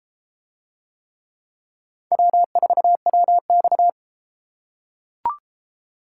Morse code, heard as a keyed tone, about 31 wpm, 700 Hz. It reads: W4WX